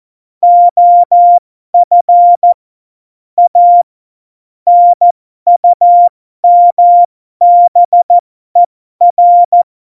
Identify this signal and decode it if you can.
Morse code, heard as a keyed tone, about 14 wpm, spaced as for 12 wpm, 700 Hz. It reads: OF A NUMBER
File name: Morsekurs / Witze